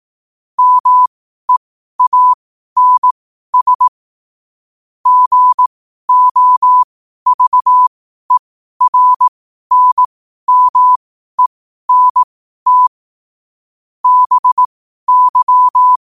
Morse code, heard as a keyed tone, about 18 wpm, spaced as for 12 wpm, 1000 Hz. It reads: MEANS GOVERNMENT BY
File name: qotdfiles / 12wpm